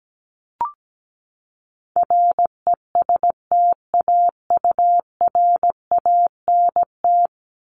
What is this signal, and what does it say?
Morse code, heard as a keyed tone, about 17 wpm, 700 Hz. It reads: RESTAURANT
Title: Top 1751-2000 Words 17wpm